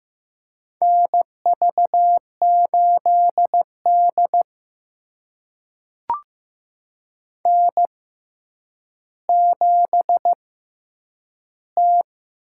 Morse code, heard as a keyed tone, about 15 wpm, spaced as for 8 wpm, 700 Hz. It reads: NV8D N7T
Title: USA Call Signs - 4 Characters - 6x Character Spacing - 15wpm